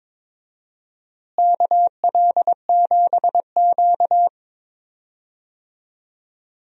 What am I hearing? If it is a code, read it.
Morse code, heard as a keyed tone, about 22 wpm, 700 Hz. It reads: KL7Q
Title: DXCC-100 Call Signs - 4 Characters - Repeated 3x - 22wpm